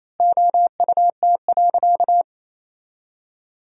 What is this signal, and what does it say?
Morse code, heard as a keyed tone, about 28 wpm, 700 Hz. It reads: OUT.
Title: 3x Word-Spacing - Sentences from Top 200 Words - 28wpm